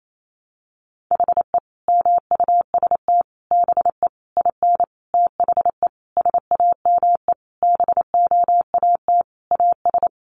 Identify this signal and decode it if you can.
Morse code, heard as a keyed tone, about 28 wpm, 700 Hz. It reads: HE MUST BE IN THE SAME BOAT AS
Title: Sentences from Top 500 Words 28wpm